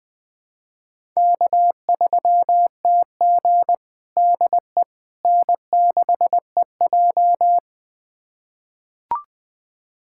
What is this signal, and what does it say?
Morse code, heard as a keyed tone, about 21 wpm, 700 Hz. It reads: K3TG DE N6EJ
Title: USA Callsign DE Callsign - 4 Characters - 20wpm